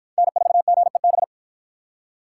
Morse code, heard as a keyed tone, about 53 wpm, 700 Hz. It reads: N4GEB